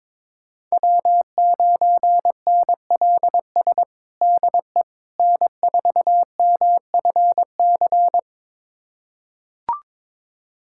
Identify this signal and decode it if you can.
Morse code, heard as a keyed tone, about 22 wpm, 700 Hz. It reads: W9NLS DE N4MFC
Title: USA Callsign DE Callsign 22wpm